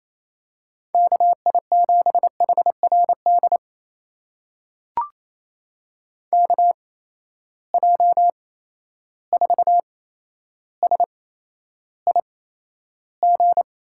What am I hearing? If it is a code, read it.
Morse code, heard as a keyed tone, about 28 wpm, spaced as for 12 wpm, 700 Hz. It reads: KI7HRD KJ4SIG
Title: USA Call Signs - 6 Characters - 8x Character Spacing - 28wpm